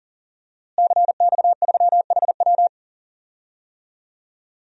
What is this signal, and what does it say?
Morse code, heard as a keyed tone, about 40 wpm, 700 Hz. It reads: CX3HW
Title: DXCC-100 Call Signs - 5 Characters - Repeated 3x - 40wpm